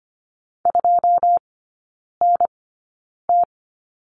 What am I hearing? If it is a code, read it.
Morse code, heard as a keyed tone, about 25 wpm, spaced as for 9 wpm, 700 Hz. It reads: 2NT
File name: Lesson 14e - Number 2 - Three Characters 25wpm